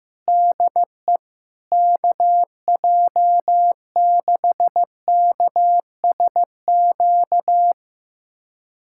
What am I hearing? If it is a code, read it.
Morse code, heard as a keyed tone, about 15 wpm, 700 Hz. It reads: DE KJ6KSQ
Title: USA Callsign DE Callsign 15wpm